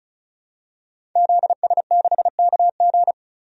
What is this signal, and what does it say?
Morse code, heard as a keyed tone, about 34 wpm, 700 Hz. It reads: ZS6KG
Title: DXCC-100 Call Signs - 5 Characters - 35wpm